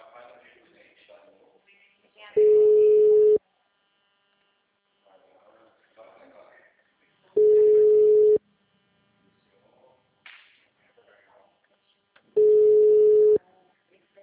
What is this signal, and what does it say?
Ringback tone (Europe, 425 Hz), with speech.
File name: out- -2007- - - .285